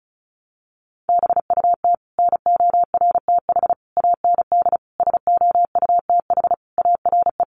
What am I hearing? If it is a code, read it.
Morse code, heard as a keyed tone, about 34 wpm, 700 Hz. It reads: BUT NORTH AND SOUTH ARE
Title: Sentences from Top 1100 Words 35wpm